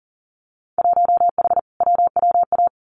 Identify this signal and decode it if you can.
Morse code, heard as a keyed tone, about 39 wpm, 700 Hz. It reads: 1H WWA
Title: Winter Field Day Exchanges Encore 40wpm